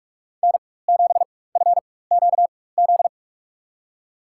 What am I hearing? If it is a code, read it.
Morse code, heard as a keyed tone, about 43 wpm, spaced as for 21 wpm, 700 Hz. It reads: N7FQZ
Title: USA Call Signs - 4x Character Spacing - 45wpm